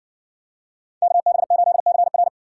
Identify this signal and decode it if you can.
Morse code, heard as a keyed tone, about 60 wpm, 700 Hz. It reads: KB8CR